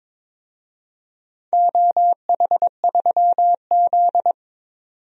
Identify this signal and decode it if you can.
Morse code, heard as a keyed tone, about 22 wpm, 700 Hz. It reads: OH3Z
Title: DXCC-100 Call Signs - 4 Characters - 22wpm